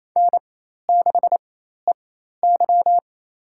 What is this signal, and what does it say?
Morse code, heard as a keyed tone, about 28 wpm, spaced as for 13 wpm, 700 Hz. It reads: N6EY
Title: USA Call Signs - 4x Character Spacing - 28wpm